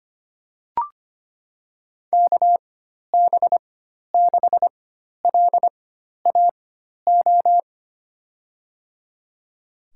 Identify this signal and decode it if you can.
Morse code, heard as a keyed tone, about 25 wpm, spaced as for 12 wpm, 700 Hz. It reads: KB6LAO
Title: USA Call Signs - 6 Characters - 4x Character Spacing - 25wpm